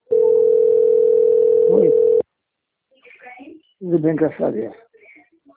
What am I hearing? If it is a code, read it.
Ringback tone (North America), 440 + 480 Hz, with speech.
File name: out- - - - - .40806